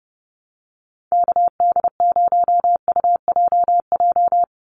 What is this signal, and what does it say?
Morse code, heard as a keyed tone, about 30 wpm, 700 Hz. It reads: KD0UJJ
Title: USA Call Signs 30wpm